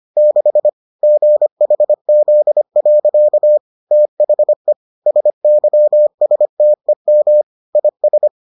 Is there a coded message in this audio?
Yes: Morse code from 0.2 to 8.3 s, heard as a keyed tone, about 25 wpm, 600 Hz. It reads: 6 GHZ. THE SYSTEM IS